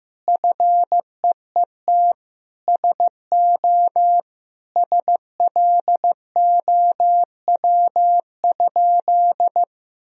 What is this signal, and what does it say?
Morse code, heard as a keyed tone, about 15 wpm, 700 Hz. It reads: FEET SO SLOW?
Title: Sentences from Top 1000 Words - Review Part 2 - 15wpm